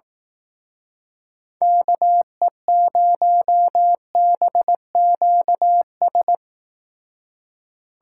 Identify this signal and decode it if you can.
Morse code, heard as a keyed tone, about 18 wpm, 700 Hz. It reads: KE0BQS
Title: Speed Racing - USA 2x3 Call Signs 15wpm